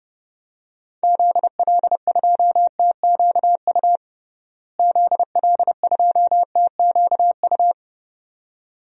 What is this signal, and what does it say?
Morse code, heard as a keyed tone, about 32 wpm, 700 Hz. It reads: ZL2TQU ZL2TQU
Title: DXCC-100 Call Signs - 6 Characters - Repeated 2x - 30wpm